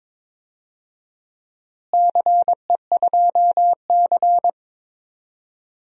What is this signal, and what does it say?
Morse code, heard as a keyed tone, about 22 wpm, 700 Hz. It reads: CE2C
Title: DXCC-100 Call Signs - 4 Characters - Repeated 2x - 22wpm